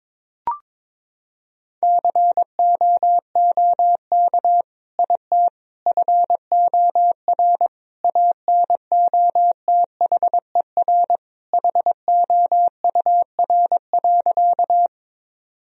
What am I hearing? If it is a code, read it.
Morse code, heard as a keyed tone, about 22 wpm, 700 Hz. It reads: COOK IT FOR ANOTHER HOUR.
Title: Sentences from Top 1000 Words - Review Part 4 - 22wpm